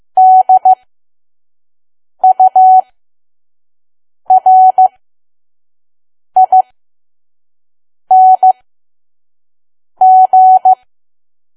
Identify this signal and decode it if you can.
Morse code, heard as a keyed tone, about 15 wpm, spaced as for 5 wpm, 750 Hz. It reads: DURING
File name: Archive / 5 WPM files / PM